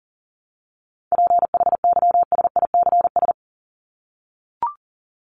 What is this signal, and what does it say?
Morse code, heard as a keyed tone, about 39 wpm, 700 Hz. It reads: PHYSICS
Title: Top 501-750 Words 40wpm